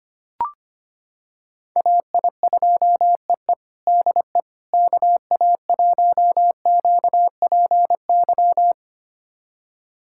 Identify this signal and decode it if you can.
Morse code, heard as a keyed tone, about 25 wpm, 700 Hz. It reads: AI2EE DE KA1QPY
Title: USA Callsign DE Callsign 25wpm